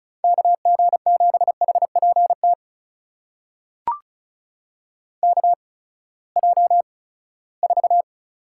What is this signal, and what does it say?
Morse code, heard as a keyed tone, about 36 wpm, spaced as for 20 wpm, 700 Hz. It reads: KG7HPT KJ4
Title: USA Call Signs - 6 Characters - 8x Character Spacing - 35wpm